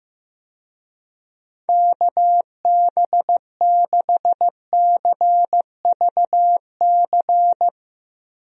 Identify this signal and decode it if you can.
Morse code, heard as a keyed tone, about 16 wpm, 700 Hz. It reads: KB6CVC
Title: USA Call Signs - 6 Characters - Repeated 2x - 15wpm